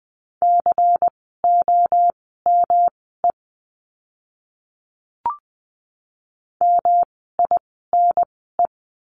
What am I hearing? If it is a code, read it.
Morse code, heard as a keyed tone, about 20 wpm, spaced as for 15 wpm, 700 Hz. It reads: COME MINE
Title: 108 Lesson 18f - Letter m - Words 20wpm